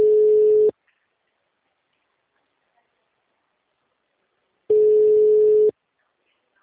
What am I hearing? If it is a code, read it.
Ringback tone (Europe, 425 Hz).